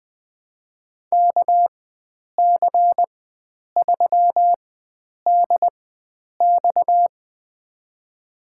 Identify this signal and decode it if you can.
Morse code, heard as a keyed tone, about 20 wpm, spaced as for 9 wpm, 700 Hz. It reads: KC3DX
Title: USA Call Signs - 4x Character Spacing - 20wpm